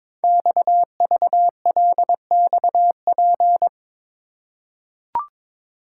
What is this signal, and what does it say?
Morse code, heard as a keyed tone, about 22 wpm, 700 Hz. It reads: XVLXP
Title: Code Groups - 5 Letters - 22wpm